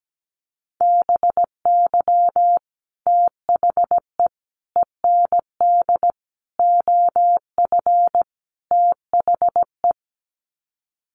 Morse code, heard as a keyed tone, about 17 wpm, 700 Hz. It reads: BY THE END OF THE
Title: Sets of 5 Words 17wpm